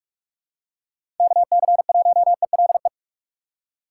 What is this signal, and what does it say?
Morse code, heard as a keyed tone, about 44 wpm, 700 Hz. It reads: KC1ELE
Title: USA Call Signs - 6 Characters - 2x Character Spacing - 45wpm